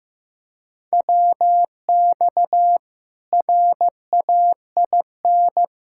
Morse code, heard as a keyed tone, about 15 wpm, 700 Hz. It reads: WX RAIN
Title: K1USN SST QSO Practice 15wpm